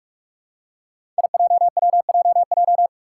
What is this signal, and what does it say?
Morse code, heard as a keyed tone, about 46 wpm, 700 Hz. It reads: IJWJJ
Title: Five Challenging Letter-Number Combinations 45wpm